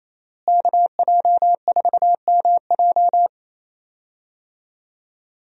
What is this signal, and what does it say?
Morse code, heard as a keyed tone, about 28 wpm, 700 Hz. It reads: KJ4MJ
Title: USA Call Signs - 5 Characters - Repeated 2x - 28wpm